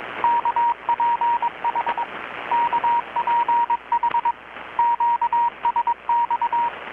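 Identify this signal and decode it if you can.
Morse code, heard as a keyed tone, about 22 wpm, 940 Hz, with radio noise. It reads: KPH KPH QSX
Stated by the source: Morse from KPH's Night of Nights XX maritime radiotelegraphy event, received in Chiapas, Mexico about 3,800 km away on a Kenwood R-600 with a grounded 10 m long-wire antenna.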